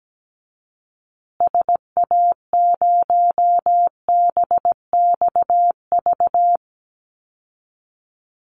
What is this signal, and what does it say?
Morse code, heard as a keyed tone, about 17 wpm, 700 Hz. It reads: SA0BXV
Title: Call Signs Repeated 3x - 17wpm